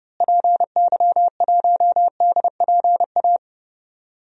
Morse code, heard as a keyed tone, about 29 wpm, 700 Hz. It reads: PY1DPA
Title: DXCC-100 Call Signs - 6 Characters - Repeated 2x - 30wpm